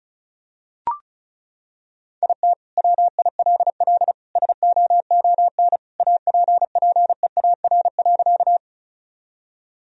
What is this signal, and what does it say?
Morse code, heard as a keyed tone, about 36 wpm, 700 Hz. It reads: IT WILL SOON APPEAR.
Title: Sentences from Top 1000 Words - Review Part 2 - 35wpm